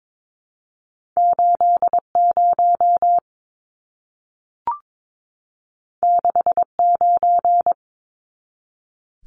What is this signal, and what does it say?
Morse code, heard as a keyed tone, about 22 wpm, 700 Hz. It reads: 80 69
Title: Numbers - 2-Digit 22wpm